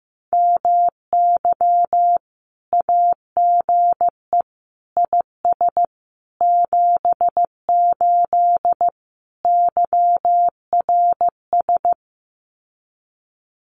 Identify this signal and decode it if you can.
Morse code, heard as a keyed tone, about 14 wpm, 700 Hz. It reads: MY AGE IS 78 YRS
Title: Common QSO Phrases Encore 15wpm